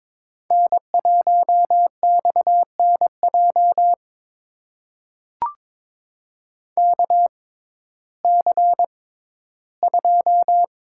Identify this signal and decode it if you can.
Morse code, heard as a keyed tone, about 22 wpm, spaced as for 13 wpm, 700 Hz. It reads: N1XNJ KC2
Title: USA Call Signs - 6x Character Spacing - 22wpm